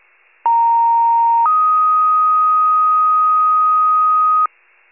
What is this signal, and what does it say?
Two-tone page, heard as a one-second tone, then a three-second tone, about 930 then 1250 Hz. It is heard from 0.5 to 4.5 s.
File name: wcalert